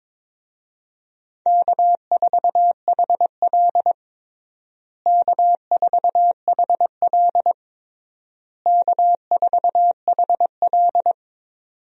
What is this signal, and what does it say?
Morse code, heard as a keyed tone, about 22 wpm, 700 Hz. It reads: K4HL K4HL K4HL
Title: USA Call Signs - 4 Characters - Repeated 3x - 22wpm